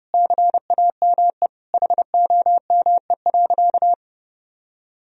Morse code, heard as a keyed tone, about 30 wpm, 700 Hz. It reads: CAME HOME.
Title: Sentences from Top 1000 Words - Review Part 4 - 30wpm